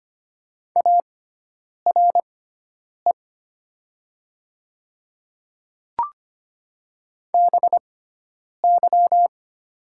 Morse code, heard as a keyed tone, about 25 wpm, spaced as for 9 wpm, 700 Hz. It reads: ARE BY
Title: 6x Character Spacing - Top 300 Words - 25wpm